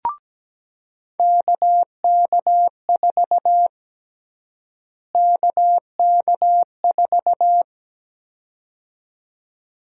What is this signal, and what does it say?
Morse code, heard as a keyed tone, about 17 wpm, 700 Hz. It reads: KK4 KK4
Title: USA Call Sign Prefixes - Repeated 2x - 17wpm